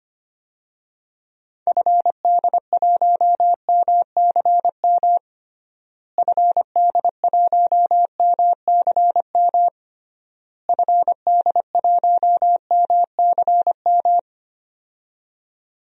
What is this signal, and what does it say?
Morse code, heard as a keyed tone, about 25 wpm, 700 Hz. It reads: FD1MCM FD1MCM FD1MCM
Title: DXCC-100 Call Signs - 6 Characters - Repeated 3x - 25wpm